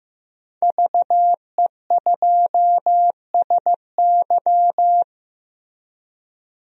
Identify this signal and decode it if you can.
Morse code, heard as a keyed tone, about 15 wpm, 700 Hz. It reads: VE2SY
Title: DXCC-100 Call Signs - 5 Characters - 15wpm